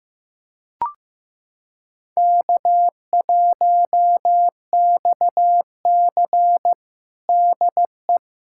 Morse code, heard as a keyed tone, about 15 wpm, 700 Hz. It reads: K1XC DE
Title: USA Callsign DE Callsign - 4 Characters - 15wpm